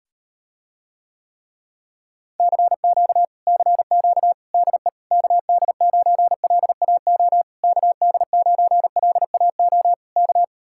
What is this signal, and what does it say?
Morse code, heard as a keyed tone, about 38 wpm, 700 Hz. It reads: CQ CQ DE KD9LAO KD9LAO K